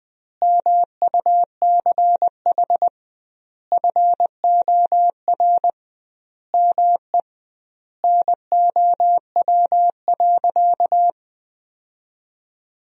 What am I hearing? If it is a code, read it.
Morse code, heard as a keyed tone, about 21 wpm, 700 Hz. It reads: MUCH FOR ME NOW.
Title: 2x Word-Spacing - Sentences from Top 200 Words - 20wpm